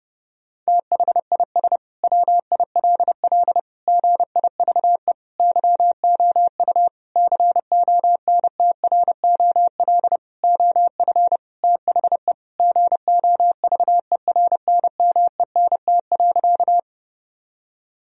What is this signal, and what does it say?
Morse code, heard as a keyed tone, about 30 wpm, 700 Hz. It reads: THIS WILL GIVE YOU CONTROL OF THE GOVERNMENT.
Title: Sentences from Top 1000 Words - Review Part 3 - 30wpm